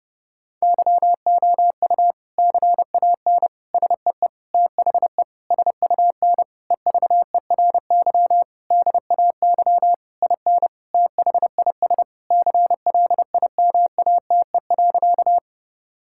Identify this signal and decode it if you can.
Morse code, heard as a keyed tone, about 30 wpm, 700 Hz. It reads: YOU CAN SEE THE SUN EVERY DAY IN THIS CLIMATE.
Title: Sentences from Top 1600 Words 30wpm